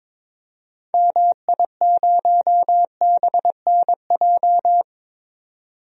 Morse code, heard as a keyed tone, about 21 wpm, 700 Hz. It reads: MI0BNJ
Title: DXCC-100 Call Signs - 6 Characters - 22wpm